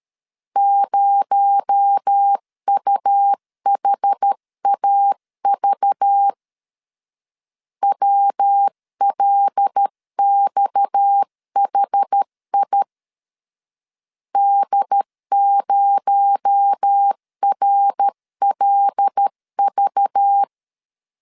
Morse code, heard as a keyed tone, about 13 wpm, 800 Hz. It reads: 0UHAV WLXHI D0RLV
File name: CW / DET / 1 6 1500C S C 12wpm 3533